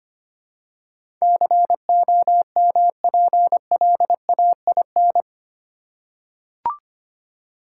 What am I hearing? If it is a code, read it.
Morse code, heard as a keyed tone, about 25 wpm, 700 Hz. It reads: COMPLAIN
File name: Eight Letter Words - 25wpm